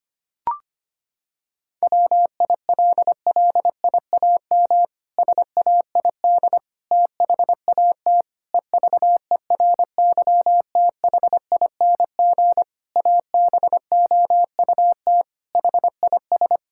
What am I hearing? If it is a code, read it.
Morse code, heard as a keyed tone, about 25 wpm, 700 Hz. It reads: WILLIAM SAID THAT EVERYTHING ABOUT HIS
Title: Pangram Challenge v2 - 25wpm